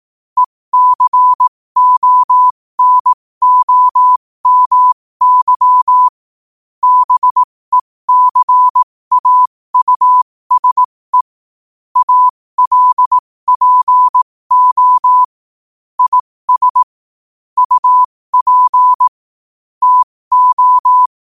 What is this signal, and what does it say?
Morse code, heard as a keyed tone, about 18 wpm, spaced as for 15 wpm, 1000 Hz. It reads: ECONOMY BECAUSE ALPO IS UP TO
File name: qotdfiles / 15wpm